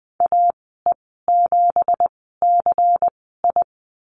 Morse code, heard as a keyed tone, about 20 wpm, spaced as for 14 wpm, 700 Hz. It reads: AE7CI
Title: 213 Lesson 34g - Number 8 - Call Signs 20wpm